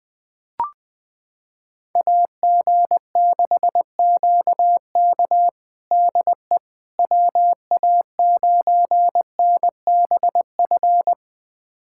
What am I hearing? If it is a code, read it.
Morse code, heard as a keyed tone, about 20 wpm, 700 Hz. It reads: AG6QK DE WA9NBF
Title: USA Callsign DE Callsign 20wpm